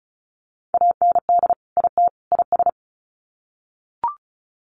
Morse code, heard as a keyed tone, about 34 wpm, 700 Hz. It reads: AND IT IS
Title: Sets of 3 Words 35wpm